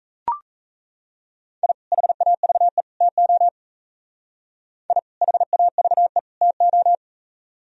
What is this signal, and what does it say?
Morse code, heard as a keyed tone, about 36 wpm, 700 Hz. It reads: I HAVE TO I HAVE TO